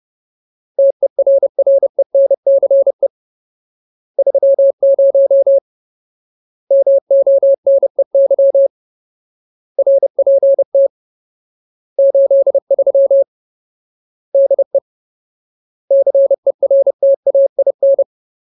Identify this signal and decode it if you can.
Morse code, heard as a keyed tone, about 30 wpm, 550 Hz. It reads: TERRENCE 30 MONEY RPT 83 DE CERTAIN
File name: Ultimate Word List / 30WPM Ultimate